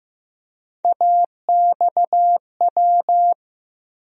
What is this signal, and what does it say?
Morse code, heard as a keyed tone, about 15 wpm, 700 Hz. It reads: AXW
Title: USA Call Sign Postfixes - 3 Letters 15wpm